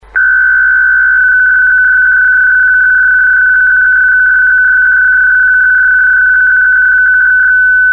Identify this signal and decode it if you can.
PSK63, about 62.5 baud.